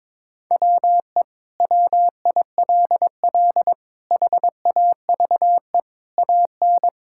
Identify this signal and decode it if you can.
Morse code, heard as a keyed tone, about 22 wpm, 700 Hz. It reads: WE WILL HAVE AN
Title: Sentences from Top 1000 Words - Review Part 5 - 22wpm